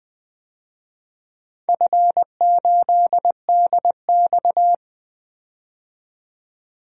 Morse code, heard as a keyed tone, about 20 wpm, 700 Hz. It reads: F8DX
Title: DXCC-100 Call Signs - 4 Characters - 20wpm